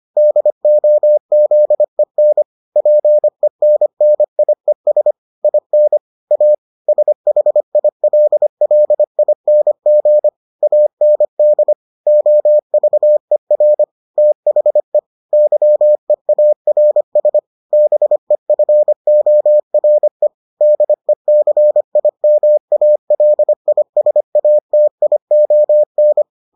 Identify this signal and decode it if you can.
Morse code, heard as a keyed tone, about 25 wpm, 600 Hz. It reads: DOZEN PENNIES IN A SHILLING AND OVER THE YEARS BEFORE DECIMALISATION